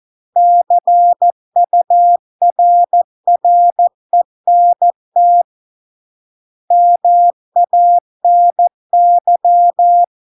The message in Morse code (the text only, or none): CURRENT MANY